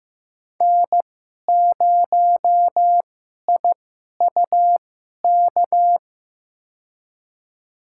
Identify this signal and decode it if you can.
Morse code, heard as a keyed tone, about 15 wpm, spaced as for 11 wpm, 700 Hz. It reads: N0IUK
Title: USA Call Signs - 5 Characters - 2x Character Spacing - 15wpm